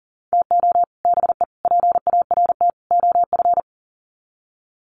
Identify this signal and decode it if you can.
Morse code, heard as a keyed tone, about 40 wpm, 700 Hz. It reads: TO BE PART OF
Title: Sets of 4 Words 40wpm